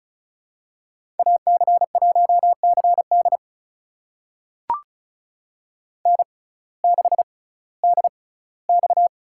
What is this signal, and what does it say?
Morse code, heard as a keyed tone, about 36 wpm, spaced as for 19 wpm, 700 Hz. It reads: AC1CD N6DX
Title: USA Call Signs - 6x Character Spacing - 35wpm